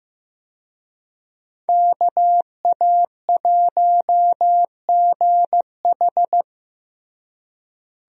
Morse code, heard as a keyed tone, about 15 wpm, 700 Hz. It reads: KA1GH
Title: USA Call Signs - 5 Characters - Repeated 3x - 15wpm